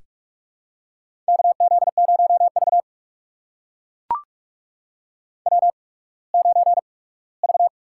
Morse code, heard as a keyed tone, about 44 wpm, spaced as for 21 wpm, 700 Hz. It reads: KZ0V W9V
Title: USA Call Signs - 8x Character Spacing - 45wpm